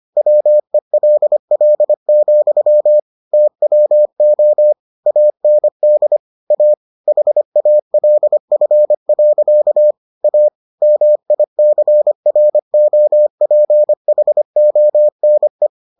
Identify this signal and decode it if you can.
Morse code, heard as a keyed tone, about 25 wpm, 600 Hz. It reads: WELL, TWO AND A HALF. A MICROPHONE